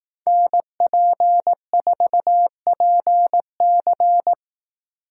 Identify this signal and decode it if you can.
Morse code, heard as a keyed tone, about 18 wpm, 700 Hz. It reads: NP4PC
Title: Speed Racing - USA 2x2 Call Signs 15wpm